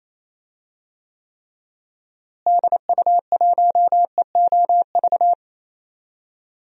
Morse code, heard as a keyed tone, about 28 wpm, 700 Hz. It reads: DU1EOV
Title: DXCC-100 Call Signs - 6 Characters - Repeated 2x - 28wpm